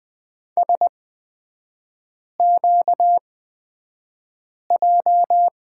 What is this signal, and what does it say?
Morse code, heard as a keyed tone, about 20 wpm, spaced as for 5 wpm, 700 Hz. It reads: SQJ
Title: 250 Lesson 40e - BK - Three Characters 20wpm